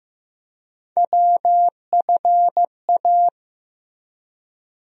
Morse code, heard as a keyed tone, about 15 wpm, 700 Hz. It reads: WFA
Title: USA Call Sign Postfixes - 3 Letters 15wpm